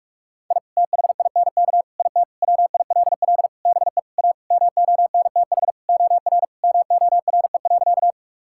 Morse code, heard as a keyed tone, about 43 wpm, 700 Hz. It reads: I THINK IT WILL BE A MONTH OR MORE.
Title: Sentences from Top 1000 Words - Review Part 5 - 45wpm